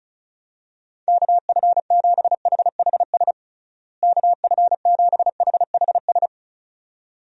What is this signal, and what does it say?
Morse code, heard as a keyed tone, about 34 wpm, 700 Hz. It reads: KF7HHS KF7HHS
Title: USA Call Signs - 6 Characters - Repeated 2x - 35wpm